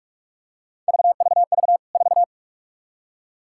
Morse code, heard as a keyed tone, about 44 wpm, 700 Hz. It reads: VVV 4
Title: K1USN SST QSO Practice 45wpm